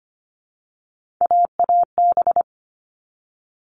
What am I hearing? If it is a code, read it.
Morse code, heard as a keyed tone, about 25 wpm, 700 Hz. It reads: AAB
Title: All Cut Numbers - 3-Digit 25wpm